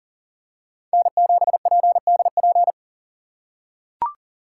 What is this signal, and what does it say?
Morse code, heard as a keyed tone, about 40 wpm, 700 Hz. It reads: N7PDP
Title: USA Call Signs - 3x Character Spacing - 40wpm